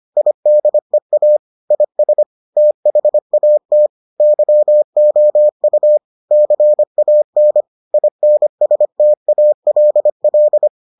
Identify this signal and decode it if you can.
Morse code, heard as a keyed tone, about 25 wpm, 600 Hz. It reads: IDEA IS THAT YOU CAN INSTALL